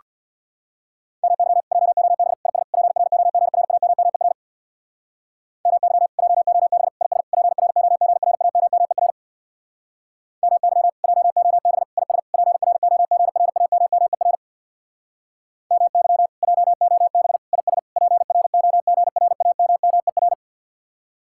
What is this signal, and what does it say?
Morse code, heard as a keyed tone, about 62 wpm, 700 Hz. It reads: MY JOB IS PROGRAMMER MY JOB IS PROGRAMMER MY JOB IS PROGRAMMER MY JOB IS PROGRAMMER
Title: Speed-Racing Common QSO Phrases Encore - 50wpm